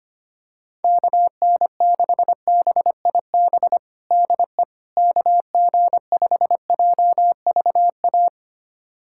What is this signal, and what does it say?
Morse code, heard as a keyed tone, about 25 wpm, 700 Hz. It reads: KN6BIB DE KG5JVA
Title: USA Callsign DE Callsign - 6 Characters - 25wpm